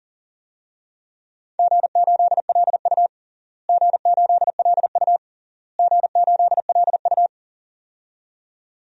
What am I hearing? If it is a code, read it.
Morse code, heard as a keyed tone, about 37 wpm, 700 Hz. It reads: G8LU G8LU G8LU
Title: DXCC-100 Call Signs - 4 Characters - Repeated 3x - 40wpm